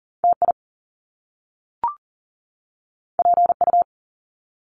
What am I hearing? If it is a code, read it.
Morse code, heard as a keyed tone, about 41 wpm, 700 Hz. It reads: TI PU
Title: Top Two Letter Combinations in Words 40wpm